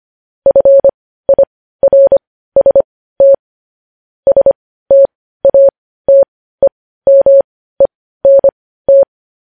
Morse code, heard as a keyed tone, about 25 wpm, spaced as for 15 wpm, 550 Hz. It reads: FIRST STATEMENT
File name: podcast / News headlines-1x-25x15